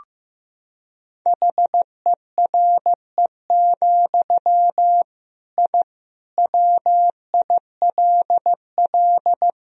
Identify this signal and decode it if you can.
Morse code, heard as a keyed tone, about 15 wpm, 700 Hz. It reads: HERE, I WILL
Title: Sentences from Top 1000 Words - Review Part 1 - 15wpm